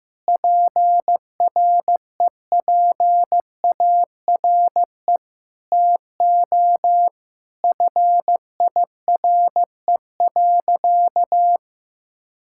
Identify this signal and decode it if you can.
Morse code, heard as a keyed tone, about 15 wpm, 700 Hz. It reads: PREPARE TO FIRE.
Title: Sentences from Top 1000 Words - Review Part 6 - 15wpm